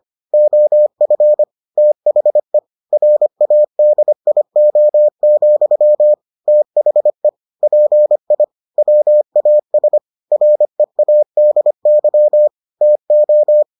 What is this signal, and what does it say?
Morse code, heard as a keyed tone, about 25 wpm, 600 Hz. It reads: OF THE RADIO, THE PI WAS READY TO